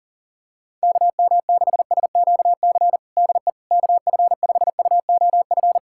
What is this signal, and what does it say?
Morse code, heard as a keyed tone, about 37 wpm, 700 Hz. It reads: KM6SQC DE KF5UOF
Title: USA Callsign DE Callsign - 6 Characters - 40wpm